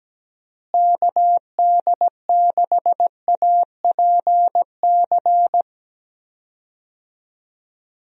Morse code, heard as a keyed tone, about 17 wpm, 700 Hz. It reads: KD6APC
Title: USA Call Signs - 6 Characters - Repeated 2x - 17wpm